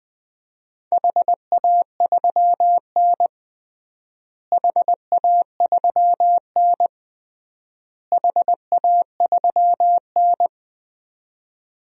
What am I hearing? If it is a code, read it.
Morse code, heard as a keyed tone, about 19 wpm, 700 Hz. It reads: HA3N HA3N HA3N
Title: DXCC-100 Call Signs - 4 Characters - Repeated 3x - 20wpm